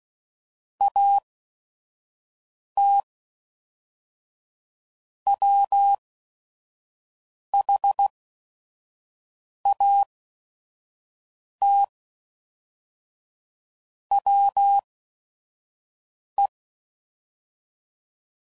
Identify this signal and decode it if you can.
Morse code, heard as a keyed tone, about 16 wpm, spaced as for 5 wpm, 800 Hz. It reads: AT WHAT WE
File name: CW Code Groups @5WPM